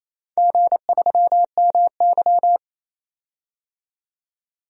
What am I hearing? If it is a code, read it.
Morse code, heard as a keyed tone, about 28 wpm, 700 Hz. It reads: G3MY
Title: DXCC-100 Call Signs - 4 Characters - 28wpm